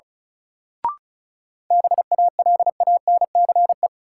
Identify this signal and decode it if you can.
Morse code, heard as a keyed tone, about 34 wpm, 700 Hz. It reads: BALANCE
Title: Sentences from Top 900 Words - New Word Drill - 35wpm